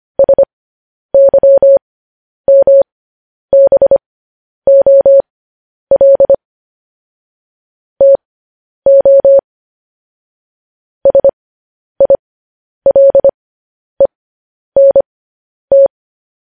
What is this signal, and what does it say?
Morse code, heard as a keyed tone, about 25 wpm, spaced as for 10 wpm, 550 Hz. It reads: SYMBOL TO SILENT
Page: News Headlines in Morse Code at 10 WPM